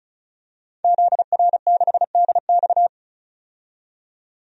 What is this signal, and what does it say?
Morse code, heard as a keyed tone, about 36 wpm, 700 Hz. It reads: ZR6DX
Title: DXCC-100 Call Signs - 5 Characters - 35wpm